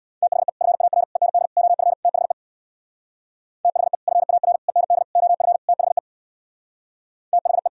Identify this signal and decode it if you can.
Morse code, heard as a keyed tone, about 65 wpm, 700 Hz. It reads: THE FAR END OF THE THE FAR END OF THE THE